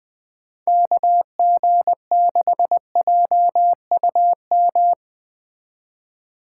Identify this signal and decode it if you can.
Morse code, heard as a keyed tone, about 21 wpm, 700 Hz. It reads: KG6JUM
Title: Speed Racing - USA 2x3 Call Signs 17wpm